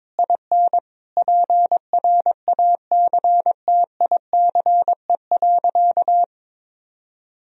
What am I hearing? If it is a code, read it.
Morse code, heard as a keyed tone, about 22 wpm, 700 Hz. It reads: IN PRACTICE.
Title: Sentences from Top 1000 Words - Review Part 4 - 22wpm